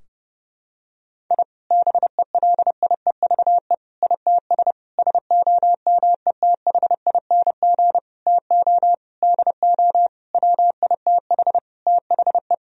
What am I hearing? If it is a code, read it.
Morse code, heard as a keyed tone, about 30 wpm, 700 Hz. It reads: I BELIEVE ITS SOMETHING TO DO WITH THE
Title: Sentences from Top 1000 Words - Review Part 3 - 30wpm